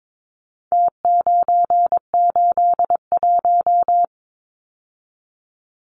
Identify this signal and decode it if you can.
Morse code, heard as a keyed tone, about 22 wpm, 700 Hz. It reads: T981
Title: Numbers - Contest Style - 4-Digit 22wpm